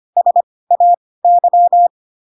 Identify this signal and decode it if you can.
Morse code, heard as a keyed tone, about 25 wpm, spaced as for 18 wpm, 700 Hz. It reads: SAY
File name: Pinocchio / Pinocchio ch1 12 25 ik1hge